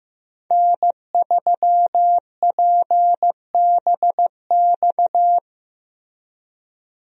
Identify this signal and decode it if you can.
Morse code, heard as a keyed tone, about 15 wpm, 700 Hz. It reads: N3PBX
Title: USA Call Signs - 5 Characters - Repeated 2x - 15wpm